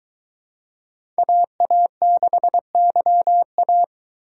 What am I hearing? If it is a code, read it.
Morse code, heard as a keyed tone, about 23 wpm, 700 Hz. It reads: AA6YA